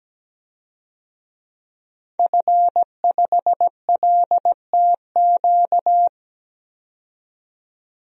Morse code, heard as a keyed tone, about 17 wpm, 700 Hz. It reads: F5LTQ
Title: DXCC-100 Call Signs - 5 Characters - 17wpm